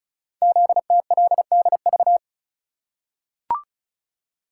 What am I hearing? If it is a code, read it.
Morse code, heard as a keyed tone, about 36 wpm, 700 Hz. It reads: ZTLDV
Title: Code Groups - 5 Letters - 35wpm